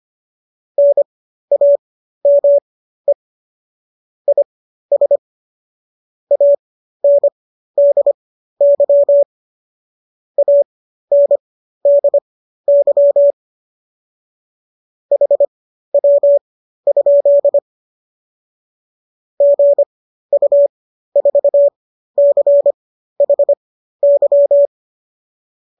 Morse code, heard as a keyed tone, about 25 wpm, spaced as for 13 wpm, 570 Hz. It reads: NAME IS ANDY ANDY HW? GU4CHY